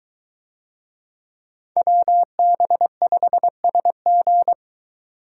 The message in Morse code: WB5SG

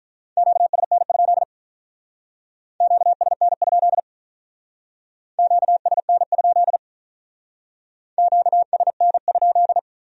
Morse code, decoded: QSN? QSN? QSN? QSN?

